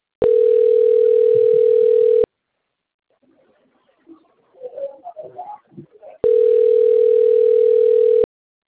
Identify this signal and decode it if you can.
Ringback tone (North America), 440 + 480 Hz.
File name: rg-1600-unknown- - - .62293